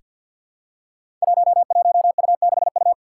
Morse code, heard as a keyed tone, about 47 wpm, 700 Hz. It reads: 11U6U
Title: Five Challenging Letter-Number Combinations 50wpm